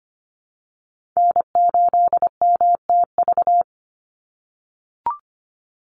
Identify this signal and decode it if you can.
Morse code, heard as a keyed tone, about 25 wpm, 700 Hz. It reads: N8MTV
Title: Call Signs - 25wpm